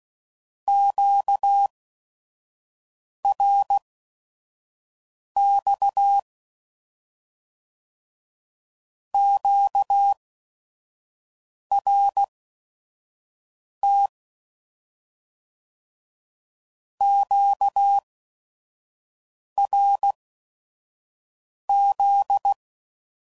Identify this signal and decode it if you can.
Morse code, heard as a keyed tone, about 16 wpm, spaced as for 5 wpm, 800 Hz. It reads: QRX QRT QRZ
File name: CW Code Groups @5WPM / Q Calls / Q Calls Once